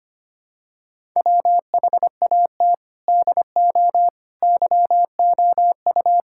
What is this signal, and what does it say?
Morse code, heard as a keyed tone, about 25 wpm, 700 Hz. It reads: WHAT DO YOU
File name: practice / 25wpm / Sentences with 3-Word Phrases - 25wpm